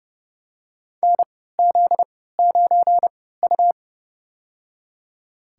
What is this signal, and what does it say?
Morse code, heard as a keyed tone, about 30 wpm, spaced as for 17 wpm, 700 Hz. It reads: NZ9U
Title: USA Call Signs - 4 Characters - 3x Character Spacing - 30wpm